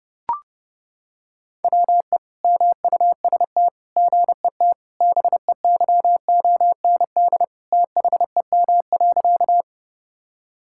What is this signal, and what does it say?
Morse code, heard as a keyed tone, about 28 wpm, 700 Hz. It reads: WE MUST GET BEYOND THEM.
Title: Sentences from Top 1000 Words - Review Part 6 - 30wpm